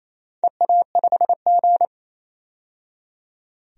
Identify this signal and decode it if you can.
Morse code, heard as a keyed tone, about 27 wpm, 690 Hz. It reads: EA5G
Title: DXCC-100 Call Signs - 4 Characters - 28wpm